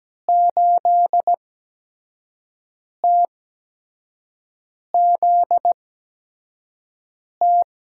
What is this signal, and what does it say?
Morse code, heard as a keyed tone, about 17 wpm, spaced as for 5 wpm, 700 Hz. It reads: 8TZT
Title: USA Call Signs - 6 Characters - 8x Character Spacing - 17wpm